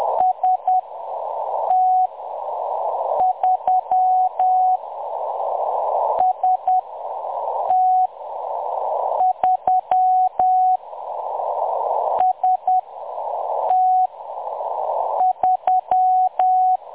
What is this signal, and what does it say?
Morse code, heard as a keyed tone, about 10 wpm, spaced as for 6 wpm, 740 Hz. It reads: ST3 ST3 ST3
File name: M23-ST3- -0858utc-10184khz